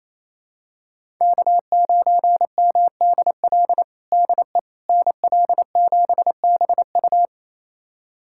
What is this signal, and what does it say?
Morse code, heard as a keyed tone, about 28 wpm, 700 Hz. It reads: K9MDL DE NL7BU
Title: USA Callsign DE Callsign - 5 Characters - 28wpm